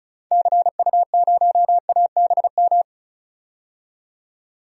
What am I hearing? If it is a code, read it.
Morse code, heard as a keyed tone, about 36 wpm, 700 Hz. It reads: CU0ABM
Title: DXCC-100 Call Signs - 6 Characters - 35wpm